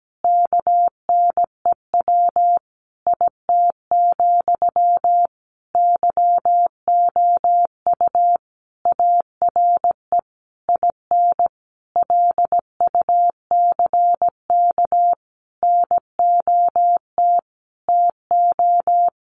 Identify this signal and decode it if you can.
Morse code, heard as a keyed tone, about 17 wpm, 700 Hz. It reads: KNEW IT, YOU ARE IN LUCK NOT TO
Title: Sentences from Top 1500 Words 17wpm